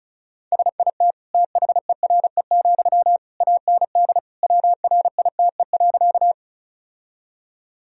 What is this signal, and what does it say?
Morse code, heard as a keyed tone, about 34 wpm, 700 Hz. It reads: SIT THERE, AND WRITE.
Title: Sentences from Top 1000 Words - Review Part 1 - 35wpm